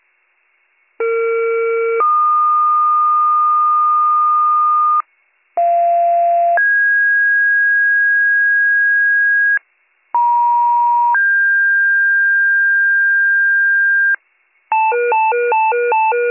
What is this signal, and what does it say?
Two-tone page, heard as a one-second tone, then a three-second tone, about 460 then 1220 Hz, about 690 then 1650 Hz, about 950 then 1600 Hz, with noise. It is heard from 1.0 to 14.2 s.